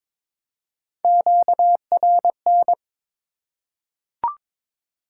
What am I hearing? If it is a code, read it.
Morse code, heard as a keyed tone, about 22 wpm, 690 Hz. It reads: QRN
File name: Top 90 QSO Elements - 3x Character Spacing - 22wpm